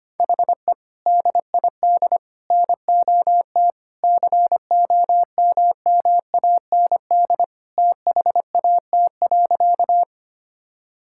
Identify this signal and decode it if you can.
Morse code, heard as a keyed tone, about 25 wpm, 700 Hz. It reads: HE DID NOT COMMAND THAT.
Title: Sentences from Top 1600 Words 25wpm